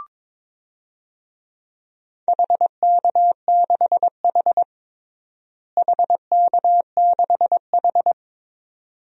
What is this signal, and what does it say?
Morse code, heard as a keyed tone, about 22 wpm, 700 Hz. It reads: HK6H HK6H
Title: DXCC-100 Call Signs - 4 Characters - Repeated 2x - 22wpm